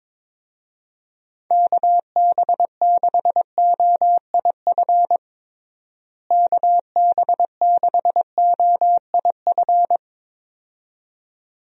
Morse code, heard as a keyed tone, about 22 wpm, 700 Hz. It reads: KB6OIF KB6OIF
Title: USA Call Signs - 6 Characters - Repeated 2x - 22wpm